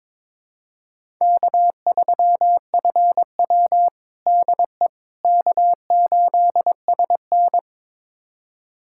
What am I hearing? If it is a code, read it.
Morse code, heard as a keyed tone, about 22 wpm, 700 Hz. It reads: K3FW DE K8SN
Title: USA Callsign DE Callsign - 4 Characters - 22wpm